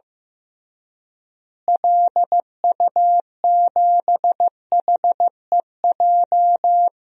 Morse code, heard as a keyed tone, about 15 wpm, 700 Hz. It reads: LU7HEJ
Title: DXCC-100 Call Signs - 6 Characters - Repeated 3x - 15wpm